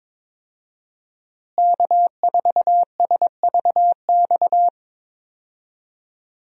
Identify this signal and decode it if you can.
Morse code, heard as a keyed tone, about 22 wpm, 700 Hz. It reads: K4SVX
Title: USA Call Signs - 5 Characters - Repeated 3x - 22wpm